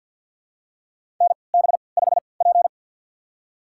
Morse code, heard as a keyed tone, about 49 wpm, spaced as for 28 wpm, 700 Hz. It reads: NB5P
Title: USA Call Signs - 4 Characters - 3x Character Spacing - 50wpm